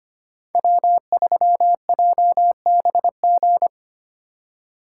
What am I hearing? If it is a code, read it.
Morse code, heard as a keyed tone, about 29 wpm, 700 Hz. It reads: W3JBG
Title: Speed Racing - USA 1x3 Call Signs 25wpm